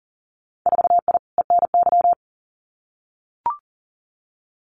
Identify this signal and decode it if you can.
Morse code, heard as a keyed tone, about 40 wpm, 700 Hz. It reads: 4I ENY
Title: Winter Field Day Exchanges 40wpm